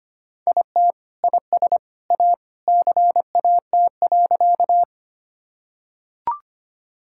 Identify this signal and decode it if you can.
Morse code, heard as a keyed tone, about 25 wpm, 700 Hz. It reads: IT IS A CAT.
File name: practice / 25wpm / Sentences with 3-Word Phrases - 25wpm